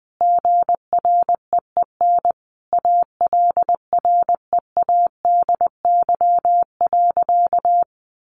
Morse code, heard as a keyed tone, about 20 wpm, 700 Hz. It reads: GREEN ALREADY.
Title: Sentences from Top 700 Words 20wpm